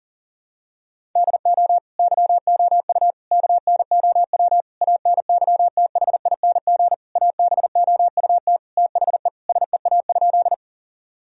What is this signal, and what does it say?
Morse code, heard as a keyed tone, about 40 wpm, 700 Hz. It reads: DO YOU KNOW ANYTHING ABOUT THE SEA?
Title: Sentences from Top 1000 Words - Review Part 1 - 40wpm